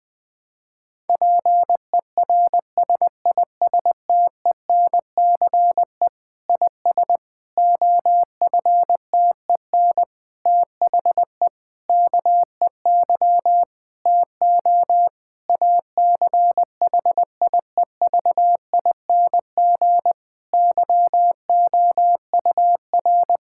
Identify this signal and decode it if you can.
Morse code, heard as a keyed tone, about 20 wpm, 700 Hz. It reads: PERSISTENCE IS OFTEN THE KEY TO ACHIEVING YOUR
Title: Difficult Sentences with Hints - 20wpm